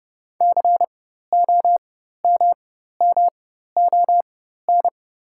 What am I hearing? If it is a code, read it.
Morse code, heard as a keyed tone, about 30 wpm, spaced as for 14 wpm, 690 Hz. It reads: COMMON
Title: 4x Character Spacing - Top 500 Words - 30wpm